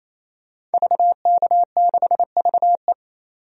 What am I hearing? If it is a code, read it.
Morse code, heard as a keyed tone, about 28 wpm, 700 Hz. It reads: VK6VE
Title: DXCC-100 Call Signs - 5 Characters - 28wpm